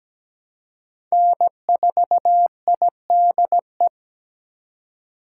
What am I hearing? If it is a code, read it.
Morse code, heard as a keyed tone, about 17 wpm, 700 Hz. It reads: N4IDE
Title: USA Call Signs - 8x Character Spacing - 17wpm